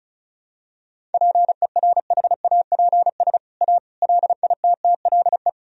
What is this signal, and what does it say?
Morse code, heard as a keyed tone, about 36 wpm, 700 Hz. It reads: PERHAPS A LITTLE
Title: Sentences from Top 1000 Words - Review Part 1 - 35wpm